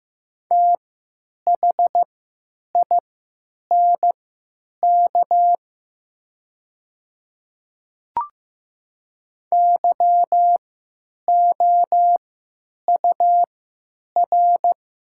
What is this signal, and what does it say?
Morse code, heard as a keyed tone, about 15 wpm, spaced as for 9 wpm, 700 Hz. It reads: THINK YOUR
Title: 3x Character Spacing - Top 200 Words - 15wpm